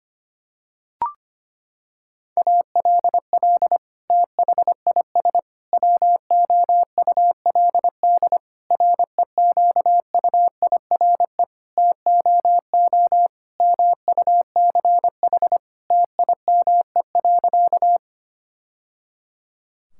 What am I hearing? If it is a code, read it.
Morse code, heard as a keyed tone, about 25 wpm, 700 Hz. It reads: ALL THIS WOULD REQUIRE TOO MUCH TIME.
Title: Sentences from Top 1000 Words - Review Part 6 - 25wpm